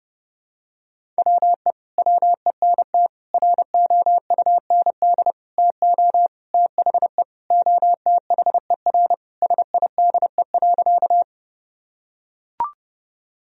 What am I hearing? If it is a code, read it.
Morse code, heard as a keyed tone, about 32 wpm, 700 Hz. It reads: WE WENT ROUND TO THE OTHER SIDE.
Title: Sentences from Top 1000 Words - Review Part 1 - 30wpm